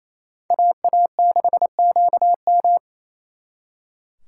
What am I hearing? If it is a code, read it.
Morse code, heard as a keyed tone, about 34 wpm, 700 Hz. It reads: AA6QM
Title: Speed Racing - USA 2x2 Call Signs 28wpm